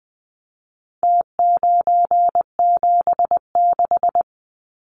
Morse code, heard as a keyed tone, about 20 wpm, 700 Hz. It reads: T976
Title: Numbers - Contest Style - 4-Digit 20wpm